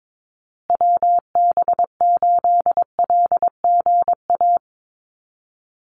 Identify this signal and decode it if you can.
Morse code, heard as a keyed tone, about 22 wpm, 700 Hz. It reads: WB8LGA
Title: USA Call Signs Repeated 2x 22wpm